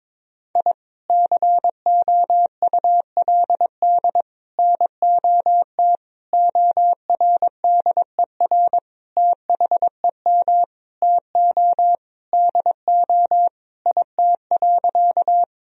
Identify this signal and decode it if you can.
Morse code, heard as a keyed tone, about 22 wpm, 700 Hz. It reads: I COULD NOT ORDER THEM TO DO IT.
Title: Sentences from Top 1000 Words - Review Part 2 - 22wpm